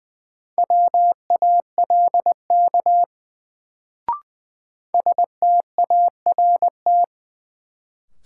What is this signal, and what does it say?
Morse code, heard as a keyed tone, about 20 wpm, 700 Hz. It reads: WALK START
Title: Sentences from Top 300 Words - New Word Drill - 20wpm